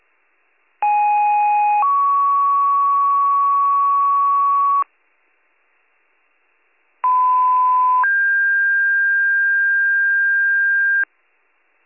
Two-tone page, heard as a one-second tone, then a three-second tone, about 830 then 1120 Hz, about 1010 then 1600 Hz. It is heard from 0.8 to 11.0 s.